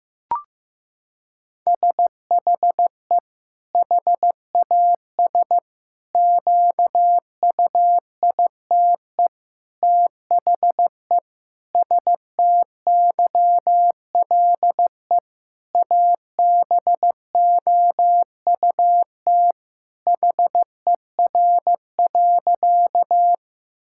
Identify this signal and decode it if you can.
Morse code, heard as a keyed tone, about 16 wpm, 700 Hz. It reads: SHE HAS QUITE THE STYLE ABOUT HER.
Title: Sentences from Top 1000 Words - Review Part 5 - 15wpm